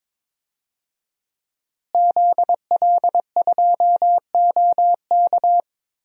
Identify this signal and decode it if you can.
Morse code, heard as a keyed tone, about 22 wpm, 700 Hz. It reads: ZL2OK
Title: DXCC-100 Call Signs - 5 Characters - Repeated 3x - 22wpm